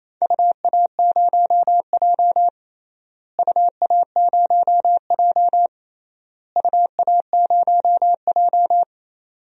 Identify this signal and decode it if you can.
Morse code, heard as a keyed tone, about 27 wpm, 700 Hz. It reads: UA0J UA0J UA0J
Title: DXCC-100 Call Signs - 4 Characters - Repeated 3x - 28wpm